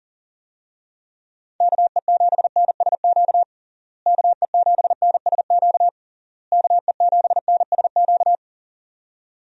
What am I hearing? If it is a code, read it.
Morse code, heard as a keyed tone, about 37 wpm, 700 Hz. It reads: KE7NSQ KE7NSQ KE7NSQ
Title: USA Call Signs - 6 Characters - Repeated 3x - 40wpm